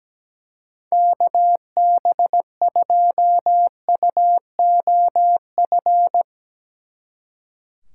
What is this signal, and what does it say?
Morse code, heard as a keyed tone, about 18 wpm, 700 Hz. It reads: KB2UOF
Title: Speed Racing - USA 2x3 Call Signs 17wpm